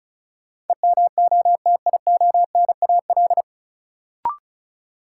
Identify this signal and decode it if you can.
Morse code, heard as a keyed tone, about 36 wpm, 700 Hz. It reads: EMOTIONAL
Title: Sentences from Top 1000 Words - Review Part 5 - New Word Drill - 35wpm